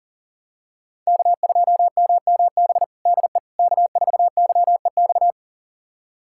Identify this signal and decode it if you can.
Morse code, heard as a keyed tone, about 40 wpm, 700 Hz. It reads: K2MMB DE K4YEX
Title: USA Callsign DE Callsign - 5 Characters - 40wpm